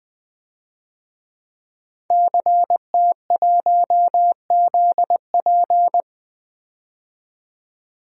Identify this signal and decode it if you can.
Morse code, heard as a keyed tone, about 20 wpm, 700 Hz. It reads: CT1ZP
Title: DXCC-100 Call Signs - 5 Characters - 20wpm